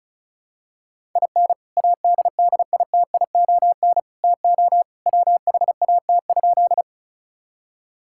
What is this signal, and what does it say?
Morse code, heard as a keyed tone, about 36 wpm, 700 Hz. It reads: IN ADDITION TO WHAT?
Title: Sentences from Top 1000 Words - Review Part 6 - 35wpm